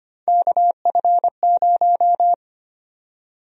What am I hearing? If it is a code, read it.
Morse code, heard as a keyed tone, about 25 wpm, 700 Hz. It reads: KF0
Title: USA Call Sign Prefixes - 25wpm